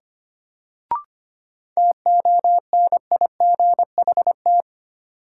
Morse code, heard as a keyed tone, about 25 wpm, 700 Hz. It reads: TONIGHT
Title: Sentences from Top 600 Words - New Word Drill - 25wpm